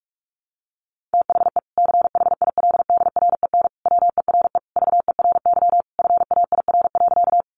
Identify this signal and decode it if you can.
Morse code, heard as a keyed tone, about 44 wpm, 700 Hz. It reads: THE CHILDREN WERE VERY FAIR.
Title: Sentences from Top 1300 Words 45wpm